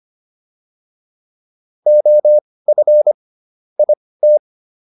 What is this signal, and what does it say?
Morse code, heard as a keyed tone, about 25 wpm, spaced as for 18 wpm, 600 Hz. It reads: OF IT